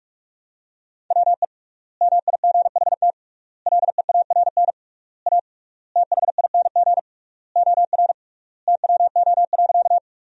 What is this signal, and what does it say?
Morse code, heard as a keyed tone, about 46 wpm, 700 Hz. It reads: WE MIGHT LEARN A THING OR TWO.
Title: 3x Word-Spacing - Sentences from Top 300 Words - 45wpm